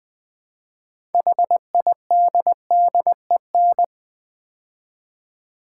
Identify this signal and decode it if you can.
Morse code, heard as a keyed tone, about 20 wpm, 700 Hz. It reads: HIDDEN
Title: Build A Word - 3 Letters At A Time - 6-Letter Words - 20wpm